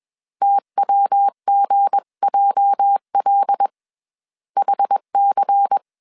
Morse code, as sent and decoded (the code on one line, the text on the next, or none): - .-- --. .--- .-.. / .... -.-.
TWGJL HC